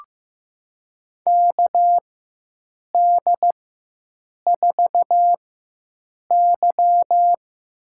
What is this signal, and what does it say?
Morse code, heard as a keyed tone, about 16 wpm, spaced as for 7 wpm, 700 Hz. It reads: KD4Y
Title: USA Call Signs - 6 Characters - 4x Character Spacing - 15wpm